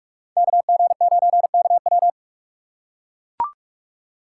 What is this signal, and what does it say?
Morse code, heard as a keyed tone, about 44 wpm, 700 Hz. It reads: KG9KW
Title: USA Call Signs - 5 Characters - 2x Character Spacing - 45wpm